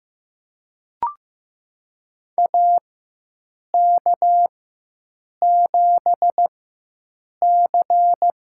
Morse code, heard as a keyed tone, about 16 wpm, spaced as for 7 wpm, 700 Hz. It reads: AK7C
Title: USA Call Signs - 4 Characters - 4x Character Spacing - 15wpm